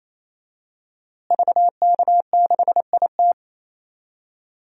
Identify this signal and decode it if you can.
Morse code, heard as a keyed tone, about 27 wpm, 700 Hz. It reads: VK6IT
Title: DXCC-100 Call Signs - 5 Characters - Repeated 3x - 28wpm